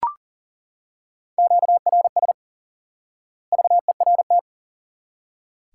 Morse code, heard as a keyed tone, about 40 wpm, 700 Hz. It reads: QRS VERT
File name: IQE Course 3 - Lesson 89 - YAGI - Two Elements - 40wpm